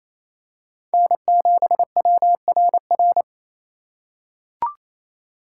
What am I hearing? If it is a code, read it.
Morse code, heard as a keyed tone, about 28 wpm, 700 Hz. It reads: N7WRR
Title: USA Call Signs - 4x Character Spacing - 28wpm